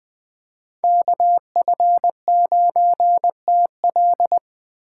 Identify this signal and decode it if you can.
Morse code, heard as a keyed tone, about 20 wpm, 700 Hz. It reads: KF9TL